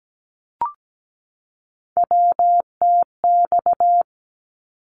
Morse code, heard as a keyed tone, about 17 wpm, 700 Hz. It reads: WTX
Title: ARRL Sections 17wpm